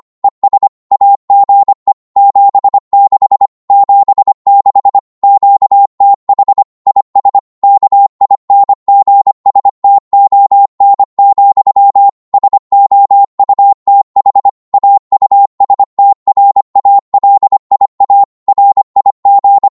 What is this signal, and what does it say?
Morse code, heard as a keyed tone, about 25 wpm, 800 Hz. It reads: ES AGE 76 76 QTH IS KINGSTON, SOUTH AUSTRALIA RIG
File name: Morse / qso-25wpm